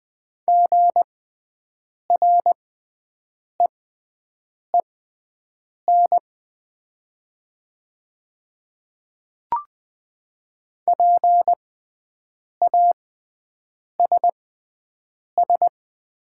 Morse code, heard as a keyed tone, about 20 wpm, spaced as for 7 wpm, 700 Hz. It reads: GREEN PASS
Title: 6x Character Spacing - Top 500 Words - 20wpm